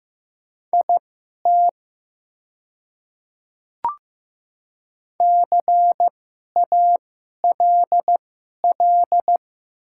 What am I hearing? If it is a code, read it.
Morse code, heard as a keyed tone, about 15 wpm, spaced as for 11 wpm, 700 Hz. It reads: IT CALL
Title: 2x Character Spacing - Top 200 Words - 15wpm